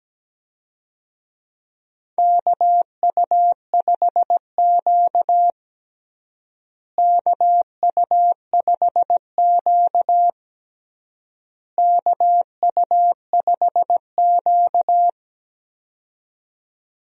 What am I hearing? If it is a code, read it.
Morse code, heard as a keyed tone, about 17 wpm, 700 Hz. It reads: KU5Q KU5Q KU5Q
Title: USA Call Signs - 4 Characters - Repeated 3x - 17wpm